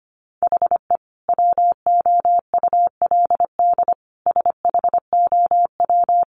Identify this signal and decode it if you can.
Morse code, heard as a keyed tone, about 25 wpm, 700 Hz. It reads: HE WOULD SHOW
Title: Sentences from Top 1300 Words 25wpm